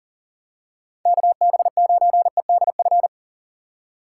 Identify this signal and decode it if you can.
Morse code, heard as a keyed tone, about 40 wpm, 700 Hz. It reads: KB9EDF
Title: USA Call Signs - 6 Characters - 6x Character Spacing - 40wpm